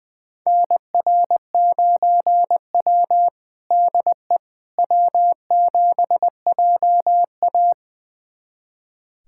Morse code, heard as a keyed tone, about 20 wpm, 700 Hz. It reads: NR9W DE W7JA
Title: USA Callsign DE Callsign - 4 Characters - 20wpm